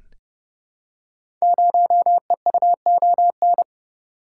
Morse code, heard as a keyed tone, about 29 wpm, 700 Hz. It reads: 0EUON
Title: Code Groups - 5 Letter:Numbers - 30wpm